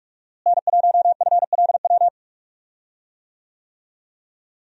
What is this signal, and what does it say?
Morse code, heard as a keyed tone, about 47 wpm, 700 Hz. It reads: N1FLW
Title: USA Call Signs - 5 Characters - Repeated 3x - 45wpm